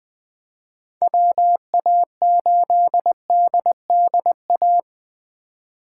Morse code, heard as a keyed tone, about 20 wpm, 700 Hz. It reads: WA8DDA
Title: USA Call Signs - 6 Characters - 3x Character Spacing - 20wpm